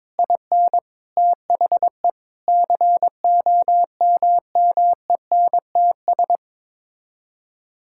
Morse code, heard as a keyed tone, about 21 wpm, 700 Hz. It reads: IN THE COMMENTS
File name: practice / 22wpm / Sets of 8 Words - 22wpm